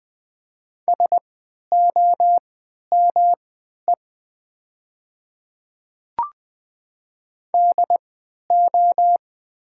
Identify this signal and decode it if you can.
Morse code, heard as a keyed tone, about 20 wpm, spaced as for 11 wpm, 700 Hz. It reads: SOME DO
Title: 3x Character Spacing - Top 100 Words - 20wpm